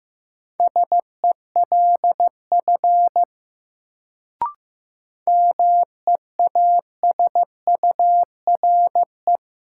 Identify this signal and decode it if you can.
Morse code, heard as a keyed tone, about 15 wpm, 700 Hz. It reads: SELF MEASURE
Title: Sentences from Top 1000 Words - Review Part 1 - New Word Drill - 15wpm